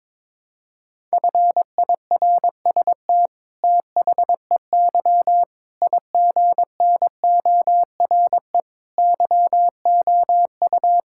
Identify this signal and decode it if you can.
Morse code, heard as a keyed tone, about 22 wpm, 700 Hz. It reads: FIRST THEY IGNORE YOU
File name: practice / 22wpm / Sentences with 8-Word Phrases - 22wpm